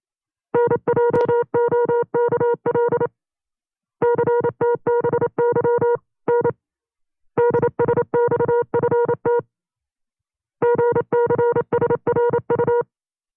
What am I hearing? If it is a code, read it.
Morse code, heard as a keyed tone, about 28 wpm, 480 Hz. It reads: NJOKL CTBY N DSXFT GCSRU
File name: CW-practice/25WPM